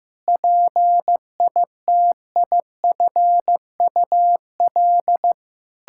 Morse code, heard as a keyed tone, about 15 wpm, 700 Hz. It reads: PITIFUL
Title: Pangram Challenge v2 - 15wpm